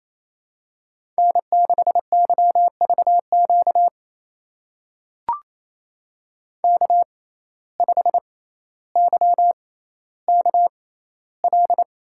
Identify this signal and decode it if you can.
Morse code, heard as a keyed tone, about 28 wpm, spaced as for 14 wpm, 700 Hz. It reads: N6YVQ K5YKL
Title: USA Call Signs - 5 Characters - 6x Character Spacing - 28wpm